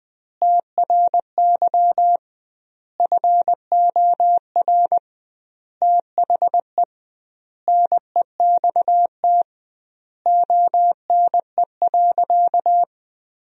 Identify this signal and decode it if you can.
Morse code, heard as a keyed tone, about 21 wpm, 700 Hz. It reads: TRY FOR THE NEXT ONE.
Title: 2x Word-Spacing - Sentences from Top 400 Words - 20wpm